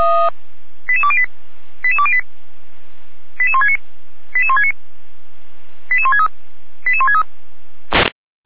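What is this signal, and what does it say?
Five-tone selective call (ZVEI), heard as quick runs of five tones, tones of about 70 ms. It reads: 88208 88208 88169 88169 88163 88163
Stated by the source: ZVEI five-tone call received by SDR, recorded for Feuerwehr Orlamünde.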